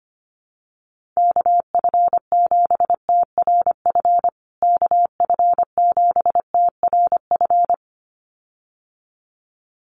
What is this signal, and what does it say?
Morse code, heard as a keyed tone, about 25 wpm, 700 Hz. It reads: KF7TRF KF7TRF
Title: USA Call Signs Repeated 2x 25wpm